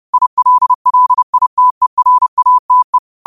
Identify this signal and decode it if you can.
Morse code, heard as a keyed tone, about 30 wpm, 1000 Hz. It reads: ILLITERATE
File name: qotdfiles / 30wpm